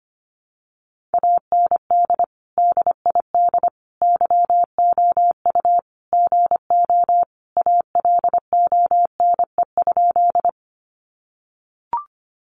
Morse code, heard as a keyed tone, about 25 wpm, 700 Hz. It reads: AND DID YOU GO ALONE?
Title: Sentences from Top 1000 Words 25wpm